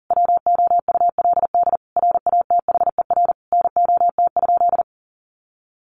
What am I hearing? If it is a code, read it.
Morse code, heard as a keyed tone, about 40 wpm, 700 Hz. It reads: WOULD RATHER NOT?
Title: Sentences from Top 700 Words 40wpm